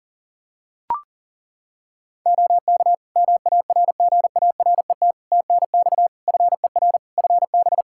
Morse code, heard as a keyed tone, about 40 wpm, 700 Hz. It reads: OK MARGARET TNX FER FB
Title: Practice QSO - Exchange 4 - Ending - 40wpm